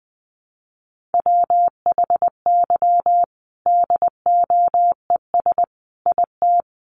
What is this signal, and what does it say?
Morse code, heard as a keyed tone, about 20 wpm, 700 Hz. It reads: WHY DOES IT
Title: Sentences from Top 1200 Words 20wpm